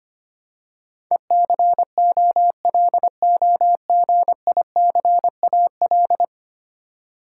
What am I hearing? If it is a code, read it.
Morse code, heard as a keyed tone, about 25 wpm, 700 Hz. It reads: ECOLOGICAL
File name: Ten Letter Words - 25wpm